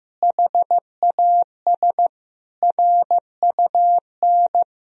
Morse code, heard as a keyed tone, about 15 wpm, 700 Hz. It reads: HAS RUN
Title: Sentences from Top 1000 Words - Review Part 1 - 15wpm